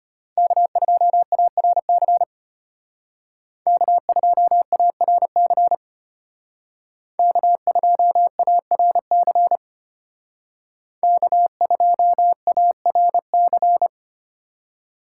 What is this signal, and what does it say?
Morse code, heard as a keyed tone, about 32 wpm, 700 Hz. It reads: K2ARC K2ARC K2ARC K2ARC